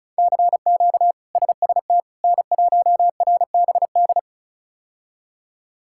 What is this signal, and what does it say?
Morse code, heard as a keyed tone, about 36 wpm, 700 Hz. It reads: CQ SST N1RBD
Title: K1USN SST QSO Practice - 35wpm